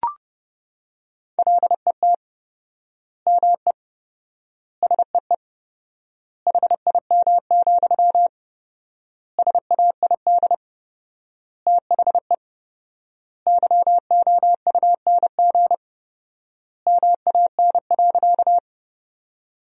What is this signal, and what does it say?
Morse code, heard as a keyed tone, about 30 wpm, 700 Hz. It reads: LET ME SEE HIM, SAID THE YOUNG MAN.
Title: 4x Word-Spacing - Sentences from Top 400 Words - 30wpm